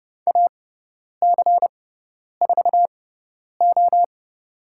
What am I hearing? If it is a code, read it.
Morse code, heard as a keyed tone, about 30 wpm, spaced as for 10 wpm, 700 Hz. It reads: AC4O
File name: Lesson 17g - Letter u - Call Signs 30wpm